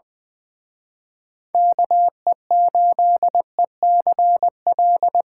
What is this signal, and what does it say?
Morse code, heard as a keyed tone, about 24 wpm, 700 Hz. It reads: KE8ECL
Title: Speed Racing - USA 2x3 Call Signs 20wpm